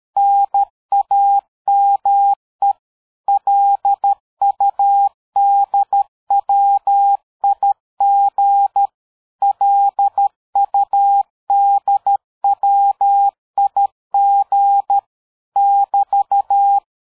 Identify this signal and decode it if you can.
Morse code, heard as a keyed tone, about 13 wpm, 800 Hz. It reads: NAME LUDWIG LUDWIG <BT>
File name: dk5ke / QSO 60BpM